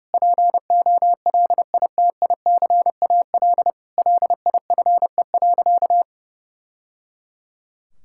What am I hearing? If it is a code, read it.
Morse code, heard as a keyed tone, about 30 wpm, 700 Hz. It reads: POLITICAL LIFE.
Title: Sentences from Top 1000 Words - Review Part 5 - 30wpm